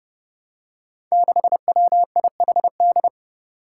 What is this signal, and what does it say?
Morse code, heard as a keyed tone, about 28 wpm, 700 Hz. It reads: 6WIHD
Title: Five Challenging Letter-Number Combinations 30wpm